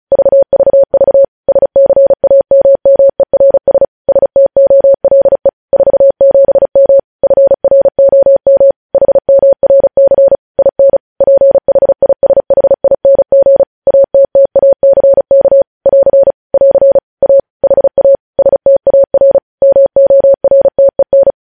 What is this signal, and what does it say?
Morse code, heard as a keyed tone, about 36 wpm, 550 Hz. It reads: VVV SCAMMERS STOLE 47M FROM HMRC IN PHISHING ATTACK <AR> <AR> A HA STAR MORTEN
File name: podcast / News headlines-1x-35x35